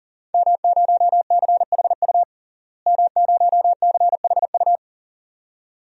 Morse code, heard as a keyed tone, about 40 wpm, 700 Hz. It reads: M0CHU M0CHU
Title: DXCC-100 Call Signs - 5 Characters - Repeated 2x - 40wpm